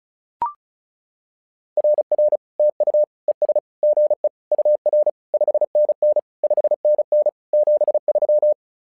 Morse code, heard as a keyed tone, about 36 wpm, 600 Hz. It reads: RR TU ES GE UR 5NN 5NN 73
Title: SOTA QSO Practice 35wpm